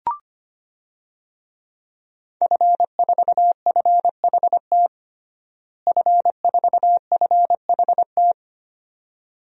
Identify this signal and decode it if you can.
Morse code, heard as a keyed tone, about 25 wpm, 700 Hz. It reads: F4FHT F4FHT
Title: DXCC-100 Call Signs - 5 Characters - Repeated 2x - 25wpm